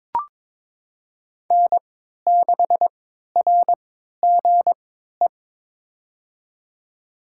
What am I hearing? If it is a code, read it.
Morse code, heard as a keyed tone, about 21 wpm, spaced as for 12 wpm, 700 Hz. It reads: N6RGE
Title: USA Call Signs - 5 Characters - 3x Character Spacing - 22wpm